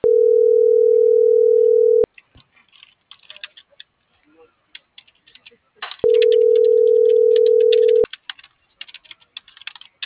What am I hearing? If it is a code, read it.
Ringback tone (North America), 440 + 480 Hz.